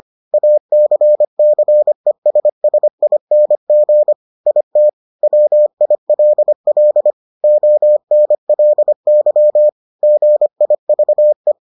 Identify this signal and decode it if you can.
Morse code, heard as a keyed tone, about 25 wpm, 600 Hz. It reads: ACCESSING IT WILL ONLY GIVE